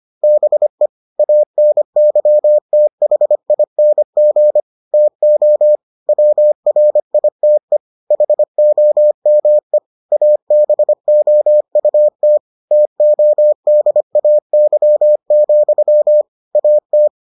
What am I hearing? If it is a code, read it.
Morse code, heard as a keyed tone, about 25 wpm, 600 Hz. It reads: BE ANYTHING TO WRITE HOME ABOUT TODAY, AT